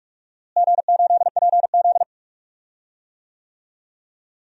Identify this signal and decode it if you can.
Morse code, heard as a keyed tone, about 42 wpm, 700 Hz. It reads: G8PZ
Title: DXCC-100 Call Signs - 4 Characters - Repeated 2x - 45wpm